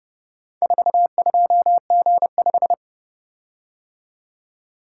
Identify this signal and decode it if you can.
Morse code, heard as a keyed tone, about 30 wpm, 700 Hz. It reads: 42G5